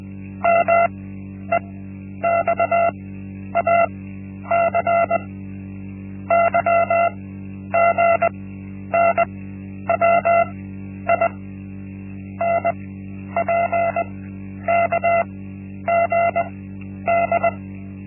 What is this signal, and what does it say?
Morse code, heard as a keyed tone, about 20 wpm, spaced as for 10 wpm, 680 Hz. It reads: MEXAC YGNWI NPKGD
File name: Linksmakalnis-1